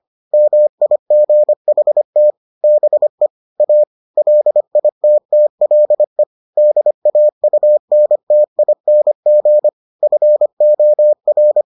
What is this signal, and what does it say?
Morse code, heard as a keyed tone, about 25 wpm, 600 Hz. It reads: MIGHT BE A LITTLE DAUNTING FOR